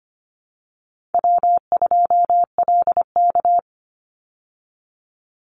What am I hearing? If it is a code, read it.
Morse code, heard as a keyed tone, about 25 wpm, 700 Hz. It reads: W2LK
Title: Call Signs Repeated 2x - 25wpm